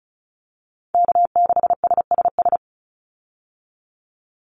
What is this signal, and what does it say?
Morse code, heard as a keyed tone, about 34 wpm, 700 Hz. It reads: K6SSS
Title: Call Signs Repeated 3x - 35wpm